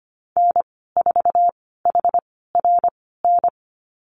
Morse code, heard as a keyed tone, about 25 wpm, spaced as for 16 wpm, 700 Hz. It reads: N4HRN
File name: Lesson 15g - Number 5 - Call Signs 25wpm